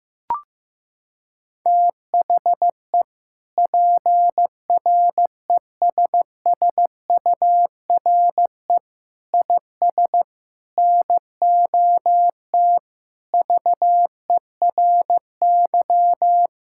Morse code, heard as a keyed tone, about 15 wpm, 700 Hz. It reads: THE PRESSURE IS NOT VERY
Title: Sentences from Top 1000 Words - Review Part 6 - 15wpm